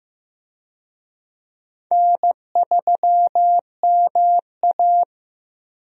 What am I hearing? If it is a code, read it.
Morse code, heard as a keyed tone, about 15 wpm, 700 Hz. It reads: N3MA
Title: USA Call Signs - 4 Characters - Repeated 3x - 15wpm